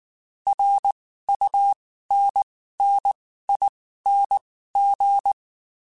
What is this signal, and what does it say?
Morse code, heard as a keyed tone, about 19 wpm, spaced as for 14 wpm, 800 Hz. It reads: RUNNING
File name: (spaced) / QSOs Spaced-mp3s / WD0FBT 1 @18 sp